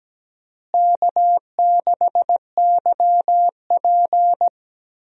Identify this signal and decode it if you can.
Morse code, heard as a keyed tone, about 17 wpm, 700 Hz. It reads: K6YP
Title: USA Call Signs - 4 Characters - Repeated 2x - 17wpm